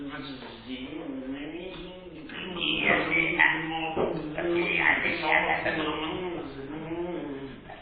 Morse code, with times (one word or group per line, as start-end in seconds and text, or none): none